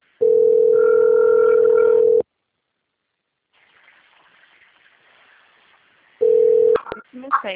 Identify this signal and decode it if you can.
Ringback tone (North America), 440 + 480 Hz.
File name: out- - - - - .28907